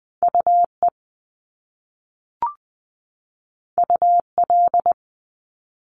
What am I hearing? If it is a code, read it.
Morse code, heard as a keyed tone, about 20 wpm, 700 Hz. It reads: UE UL
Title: Top Two Letter Combinations in Words 20wpm